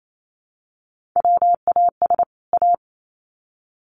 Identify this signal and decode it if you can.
Morse code, heard as a keyed tone, about 28 wpm, 700 Hz. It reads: WAS A
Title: Sets of 2 Words 28wpm